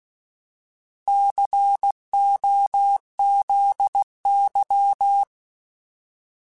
Morse code, heard as a keyed tone, about 16 wpm, 800 Hz. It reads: COZY
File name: Alphabetical / Alphabetical mp3s / Cs @15